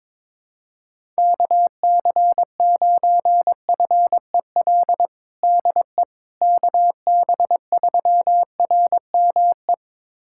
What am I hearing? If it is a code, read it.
Morse code, heard as a keyed tone, about 22 wpm, 700 Hz. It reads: KC9FEL DE KB3RME
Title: USA Callsign DE Callsign - 6 Characters - 22wpm